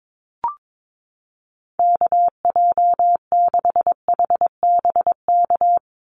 Morse code, heard as a keyed tone, about 22 wpm, 700 Hz. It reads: KJ6HBK
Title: USA Call Signs 22wpm